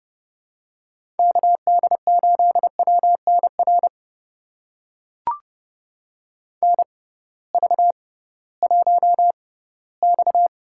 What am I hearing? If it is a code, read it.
Morse code, heard as a keyed tone, about 30 wpm, spaced as for 18 wpm, 700 Hz. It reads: KD8WNR NV1X
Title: USA Call Signs - 6x Character Spacing - 30wpm